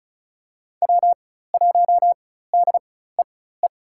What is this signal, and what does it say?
Morse code, heard as a keyed tone, about 36 wpm, spaced as for 16 wpm, 700 Hz. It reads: W1DEE
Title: USA Call Signs - 5 Characters - 4x Character Spacing - 35wpm